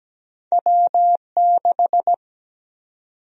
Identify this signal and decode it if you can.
Morse code, heard as a keyed tone, about 17 wpm, 700 Hz. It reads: W6
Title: USA Call Sign Prefixes - 17wpm